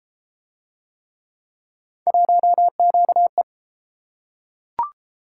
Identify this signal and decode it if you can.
Morse code, heard as a keyed tone, about 33 wpm, 700 Hz. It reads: 1QE